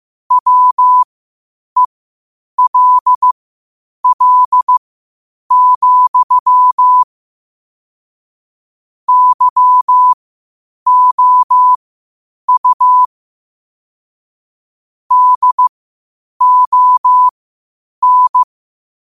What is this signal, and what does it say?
Morse code, heard as a keyed tone, about 15 wpm, spaced as for 8 wpm, 1000 Hz. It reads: WELL, YOU DON
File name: qotdfiles / 8wpm